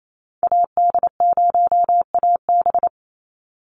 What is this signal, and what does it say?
Morse code, heard as a keyed tone, about 28 wpm, 700 Hz. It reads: AD0AB
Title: Call Signs - 28wpm